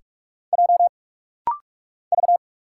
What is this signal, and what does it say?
Morse code, heard as a keyed tone, about 44 wpm, 700 Hz. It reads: J V
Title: Single Letter-Number - Rapid-Fire 45wpm